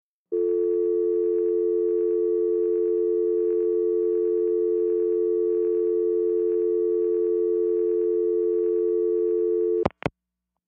Dial tone (North America), 350 + 440 Hz.
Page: Telephone dial tone, click at head